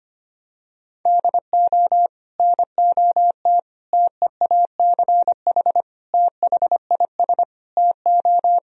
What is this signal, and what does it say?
Morse code, heard as a keyed tone, about 25 wpm, 700 Hz. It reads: DO NOT TEACH THIS TO
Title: Sentences from Top 1000 Words - Review Part 2 - 25wpm